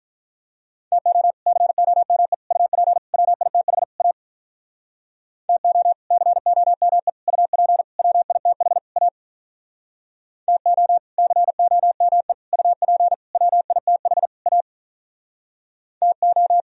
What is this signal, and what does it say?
Morse code, heard as a keyed tone, about 44 wpm, 700 Hz. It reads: TO COME UP WITH A TO COME UP WITH A TO COME UP WITH A TO